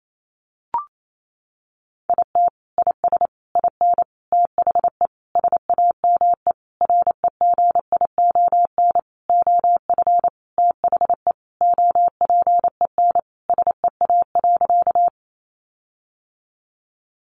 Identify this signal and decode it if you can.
Morse code, heard as a keyed tone, about 28 wpm, 700 Hz. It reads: IT IS IN THE SAME REGION OF THE OPEN SEA.
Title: Sentences from Top 1500 Words 28wpm